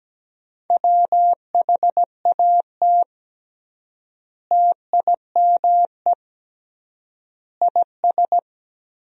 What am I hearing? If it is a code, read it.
Morse code, heard as a keyed tone, about 17 wpm, 700 Hz. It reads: WHAT TIME IS
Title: 3x Word-Spacing - Sentences from Top 100 Words - 17wpm